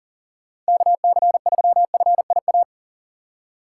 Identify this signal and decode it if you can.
Morse code, heard as a keyed tone, about 40 wpm, 700 Hz. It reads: KC3FIA
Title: USA Call Signs - 2x Character Spacing - 40wpm